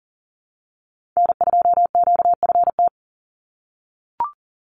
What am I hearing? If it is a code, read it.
Morse code, heard as a keyed tone, about 40 wpm, 700 Hz. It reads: N2QFT
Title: USA Call Signs 40wpm